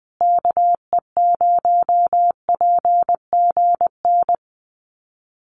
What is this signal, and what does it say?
Morse code, heard as a keyed tone, about 20 wpm, 700 Hz. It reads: KE0PGN
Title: USA Call Signs Repeated 3x 20wpm